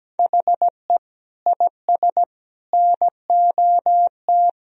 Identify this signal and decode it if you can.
Morse code, heard as a keyed tone, about 17 wpm, 700 Hz. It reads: HE IS NOT
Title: Sentences from Top 1000 Words - Review Part 1 - 17wpm